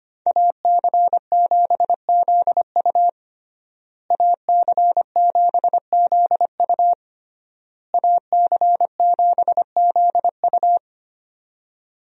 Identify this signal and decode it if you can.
Morse code, heard as a keyed tone, about 25 wpm, 700 Hz. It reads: AC7ZU AC7ZU AC7ZU
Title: USA Call Signs - 5 Characters - Repeated 3x - 25wpm